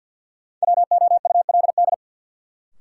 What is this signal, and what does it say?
Morse code, heard as a keyed tone, about 53 wpm, 700 Hz. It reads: WOULD
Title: Sentences from Top 100 Words - New Word Drill - 50wpm